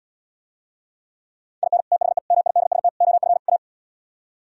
Morse code, heard as a keyed tone, about 77 wpm, 700 Hz. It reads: IN THE MIDST OF A